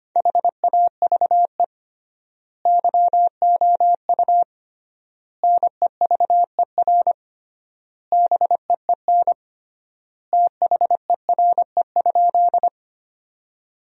Morse code, heard as a keyed tone, about 25 wpm, 700 Hz. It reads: HAVE YOU NEVER BEEN THERE?
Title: 3x Word-Spacing - Sentences from Top 300 Words - 25wpm